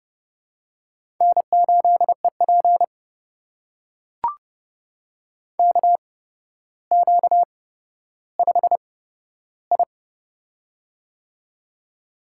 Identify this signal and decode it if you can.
Morse code, heard as a keyed tone, about 30 wpm, spaced as for 13 wpm, 700 Hz. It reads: N8EP KQ5I
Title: USA Call Signs - 4 Characters - 8x Character Spacing - 30wpm